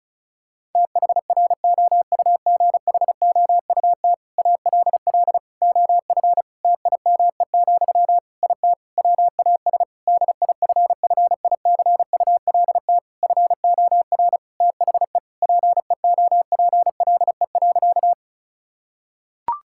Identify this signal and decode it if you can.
Morse code, heard as a keyed tone, about 34 wpm, 700 Hz. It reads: THROUGHOUT ALL OF TIME, IT WAS DIFFICULT FOR THE PEOPLE.
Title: Sentences from Top 1000 Words - Review Part 5 - 35wpm